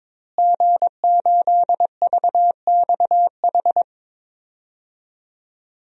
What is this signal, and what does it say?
Morse code, heard as a keyed tone, about 22 wpm, 700 Hz. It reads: G8VXH
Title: DXCC-100 Call Signs - 5 Characters - Repeated 2x - 22wpm